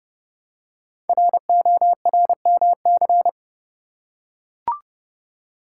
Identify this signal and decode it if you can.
Morse code, heard as a keyed tone, about 30 wpm, 700 Hz. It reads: RORMC
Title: Code Groups - 5 Letters - 30wpm